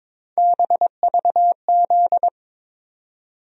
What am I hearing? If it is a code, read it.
Morse code, heard as a keyed tone, about 22 wpm, 700 Hz. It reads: BVZ
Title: USA Call Sign Postfixes - 3 Letters 22wpm